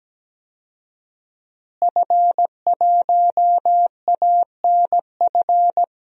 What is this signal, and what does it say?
Morse code, heard as a keyed tone, about 17 wpm, 700 Hz. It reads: F1ANF
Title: DXCC-100 Call Signs - 5 Characters - Repeated 3x - 17wpm